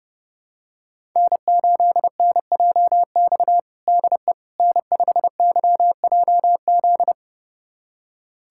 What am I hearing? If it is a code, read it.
Morse code, heard as a keyed tone, about 30 wpm, 700 Hz. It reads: N8NJX DE N5YJZ
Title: USA Callsign DE Callsign - 5 Characters - 30wpm